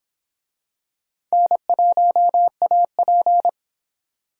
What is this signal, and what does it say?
Morse code, heard as a keyed tone, about 26 wpm, 700 Hz. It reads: N1AP